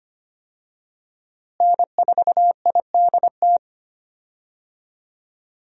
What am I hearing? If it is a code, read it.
Morse code, heard as a keyed tone, about 25 wpm, 700 Hz. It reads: N4IDT
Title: USA Call Signs - 5 Characters - Repeated 3x - 25wpm